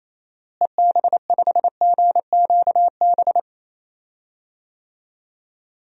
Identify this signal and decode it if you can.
Morse code, heard as a keyed tone, about 28 wpm, 700 Hz. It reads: EB5GQB
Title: DXCC-100 Call Signs - 6 Characters - Repeated 3x - 28wpm